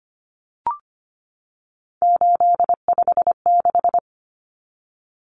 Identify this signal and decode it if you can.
Morse code, heard as a keyed tone, about 25 wpm, 700 Hz. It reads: 856
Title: Numbers - 3-Digit 25wpm